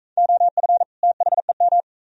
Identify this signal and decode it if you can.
Morse code, heard as a keyed tone, about 40 wpm, 690 Hz. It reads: OF THEM